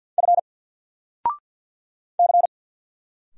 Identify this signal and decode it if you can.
Morse code, heard as a keyed tone, about 50 wpm, 700 Hz. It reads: F C
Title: Single Letter-Number 50wpm